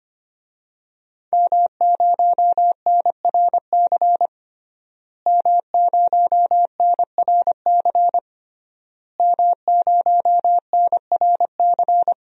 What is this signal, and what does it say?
Morse code, heard as a keyed tone, about 25 wpm, 700 Hz. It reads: M0NRC M0NRC M0NRC
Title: DXCC-100 Call Signs - 5 Characters - Repeated 3x - 25wpm